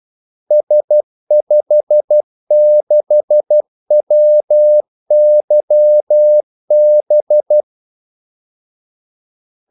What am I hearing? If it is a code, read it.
Morse code, heard as a keyed tone, about 12 wpm, 600 Hz. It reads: S56WYB
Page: Kaldesignaler 12 wpm